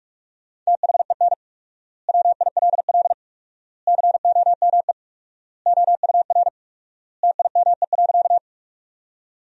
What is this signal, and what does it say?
Morse code, heard as a keyed tone, about 46 wpm, 700 Hz. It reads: THEN WILL COME OUR TIME.
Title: 4x Word-Spacing - Sentences from Top 200 Words - 45wpm